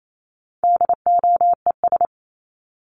Morse code, heard as a keyed tone, about 28 wpm, 700 Hz. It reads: DOES
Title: ES:S Suffix - Top 100 Words with Base Words 28wpm